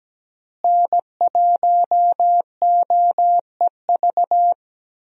Morse code, heard as a keyed tone, about 17 wpm, 700 Hz. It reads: N1OEV
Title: USA Call Signs - 5 Characters - Repeated 3x - 17wpm